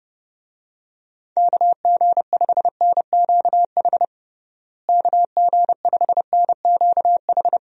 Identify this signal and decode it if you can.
Morse code, heard as a keyed tone, about 32 wpm, 700 Hz. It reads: KG5NQH KG5NQH
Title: USA Call Signs - 6 Characters - Repeated 2x - 30wpm